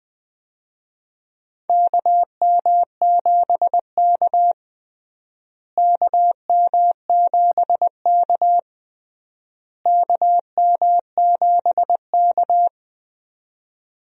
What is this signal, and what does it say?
Morse code, heard as a keyed tone, about 20 wpm, 700 Hz. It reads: KM7K KM7K KM7K
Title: USA Call Signs - 4 Characters - Repeated 3x - 20wpm